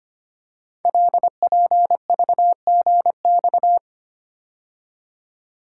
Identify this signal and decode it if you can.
Morse code, heard as a keyed tone, about 25 wpm, 700 Hz. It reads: LPVGX
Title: Code Groups - 5 Letters - 25wpm